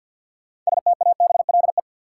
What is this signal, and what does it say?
Morse code, heard as a keyed tone, about 50 wpm, 700 Hz. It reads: STABLE
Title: Sentences from Top 1500 Words - New Word Drill Alt - 50wpm